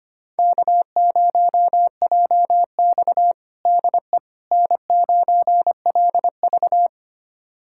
Morse code, heard as a keyed tone, about 25 wpm, 700 Hz. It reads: K0JX DE N9LV
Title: USA Callsign DE Callsign - 4 Characters - 25wpm